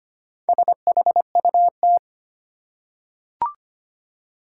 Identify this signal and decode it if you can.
Morse code, heard as a keyed tone, about 25 wpm, 700 Hz. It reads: SHUT
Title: Four Letter Words - 25wpm